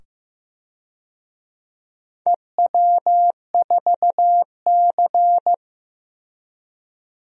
Morse code, heard as a keyed tone, about 15 wpm, 700 Hz. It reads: EW4C
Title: DXCC-100 Call Signs - 4 Characters - 15wpm